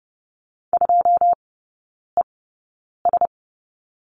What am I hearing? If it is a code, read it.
Morse code, heard as a keyed tone, about 30 wpm, spaced as for 9 wpm, 700 Hz. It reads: 2ES
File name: 081 Lesson 14e - Number 2 - Three Characters 30wpm